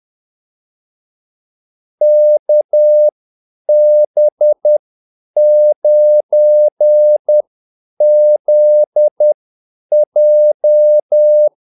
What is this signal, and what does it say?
Morse code, heard as a keyed tone, about 10 wpm, spaced as for 8 wpm, 600 Hz. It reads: KB9ZJ
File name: Callsigns-10-8wpm